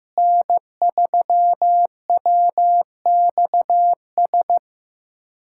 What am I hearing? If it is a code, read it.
Morse code, heard as a keyed tone, about 16 wpm, 700 Hz. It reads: N3WXS
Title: USA Callsign DE Callsign - 5 Characters - 15wpm